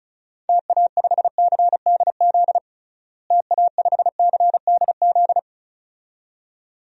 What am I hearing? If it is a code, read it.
Morse code, heard as a keyed tone, about 36 wpm, 700 Hz. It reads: TA5CDZ TA5CDZ
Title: DXCC-100 Call Signs - 6 Characters - Repeated 2x - 35wpm